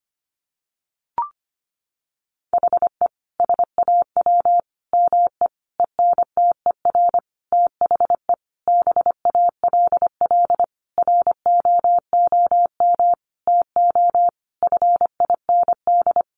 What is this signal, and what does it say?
Morse code, heard as a keyed tone, about 25 wpm, 700 Hz. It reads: HE SAW ME ENTER THE BALL ROOM TO FIND
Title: Sentences from Top 1300 Words 25wpm